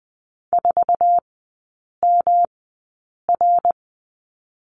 Morse code, heard as a keyed tone, about 20 wpm, spaced as for 8 wpm, 700 Hz. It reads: 4MR
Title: 107 Lesson 18e - Letter m - Three Characters 20wpm